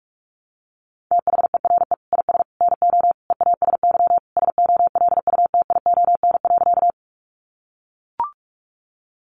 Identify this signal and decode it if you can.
Morse code, heard as a keyed tone, about 44 wpm, 700 Hz. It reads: THERE IS NO EASY SOLUTION.
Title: Sentences from Top 1300 Words 45wpm